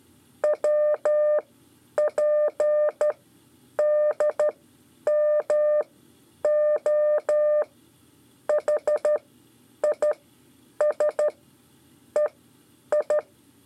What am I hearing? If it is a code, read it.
Morse code, heard as a keyed tone, about 13 wpm, spaced as for 8 wpm, 600 Hz. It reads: WPDMOHISEI
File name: 5 LEZIONE RIPASSO 1